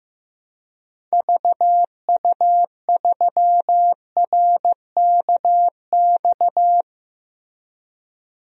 Morse code, heard as a keyed tone, about 15 wpm, 700 Hz. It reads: VU3RKX
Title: DXCC-100 Call Signs - 6 Characters - Repeated 3x - 15wpm